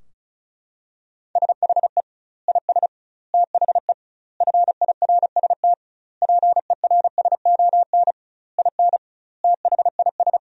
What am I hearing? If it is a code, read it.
Morse code, heard as a keyed tone, about 34 wpm, 700 Hz. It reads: SHE IS THE FIRST PERSON IN THIS
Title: 2x Word-Spacing - Sentences from Top 400 Words - 35wpm